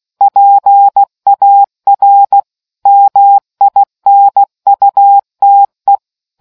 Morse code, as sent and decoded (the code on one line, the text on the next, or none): .--. .- .-. / -- .. -. ..- - .
PAR MINUTE